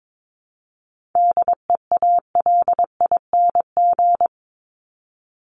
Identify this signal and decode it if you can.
Morse code, heard as a keyed tone, about 22 wpm, 700 Hz. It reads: DEALING
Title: ING Suffix - Top 100 Words with Base Words 22wpm